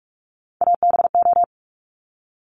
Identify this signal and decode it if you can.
Morse code, heard as a keyed tone, about 46 wpm, 690 Hz. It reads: ABO
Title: Top Three Letter Combinations in Words 45wpm